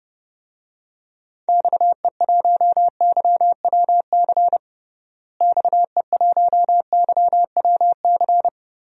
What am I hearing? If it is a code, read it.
Morse code, heard as a keyed tone, about 30 wpm, 700 Hz. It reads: XE1YWC XE1YWC
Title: DXCC-100 Call Signs - 6 Characters - Repeated 3x - 30wpm